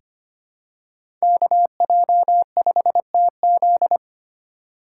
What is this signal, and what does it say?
Morse code, heard as a keyed tone, about 25 wpm, 700 Hz. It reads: KJ5TZ
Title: USA Call Signs - 5 Characters - 2x Character Spacing - 25wpm